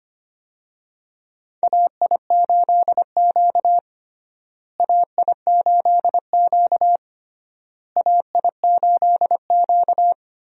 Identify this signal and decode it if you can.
Morse code, heard as a keyed tone, about 25 wpm, 700 Hz. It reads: AI8Q AI8Q AI8Q
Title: USA Call Signs - 4 Characters - Repeated 3x - 25wpm